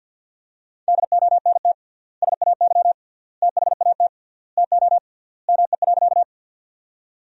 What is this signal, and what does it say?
Morse code, heard as a keyed tone, about 51 wpm, 700 Hz. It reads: DONT SAY THAT TO ME.
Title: 3x Word-Spacing - Sentences from Top 300 Words - 50wpm